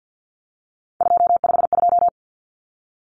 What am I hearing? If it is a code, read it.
Morse code, heard as a keyed tone, about 55 wpm, 700 Hz. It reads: 252
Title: Common Cut Numbers - 3-Digit 50wpm